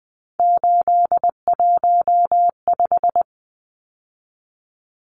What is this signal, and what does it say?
Morse code, heard as a keyed tone, about 20 wpm, 700 Hz. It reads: 815
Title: Numbers - 3-Digit 20wpm